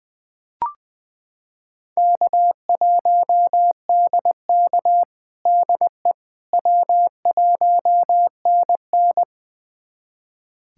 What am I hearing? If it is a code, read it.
Morse code, heard as a keyed tone, about 21 wpm, 700 Hz. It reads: K1DK DE W1NN
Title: USA Callsign DE Callsign - 4 Characters - 20wpm